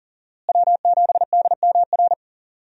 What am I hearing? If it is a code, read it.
Morse code, heard as a keyed tone, about 39 wpm, 700 Hz. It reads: W7DMR
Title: USA Call Signs - 5 Characters - Repeated 3x - 40wpm